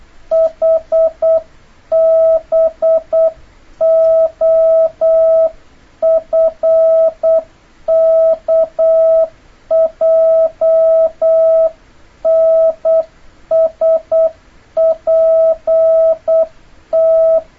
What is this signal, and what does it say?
Morse code, heard as a keyed tone, about 8 wpm, 650 Hz. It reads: HBOFKJNSPT
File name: jh8cbh / 2026cw